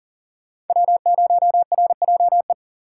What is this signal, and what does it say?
Morse code, heard as a keyed tone, about 50 wpm, 700 Hz. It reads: W0RJE